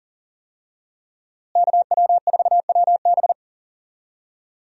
Morse code, heard as a keyed tone, about 40 wpm, 700 Hz. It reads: KW4WB
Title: USA Call Signs - 5 Characters - Repeated 2x - 40wpm